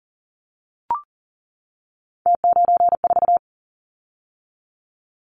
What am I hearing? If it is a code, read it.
Morse code, heard as a keyed tone, about 41 wpm, 700 Hz. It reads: T94
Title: Callsign Prefixes for Top 100 DXCC - 40wpm